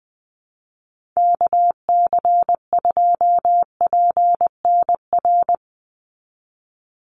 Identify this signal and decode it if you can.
Morse code, heard as a keyed tone, about 20 wpm, 700 Hz. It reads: KC2PNR
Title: USA Call Signs 20wpm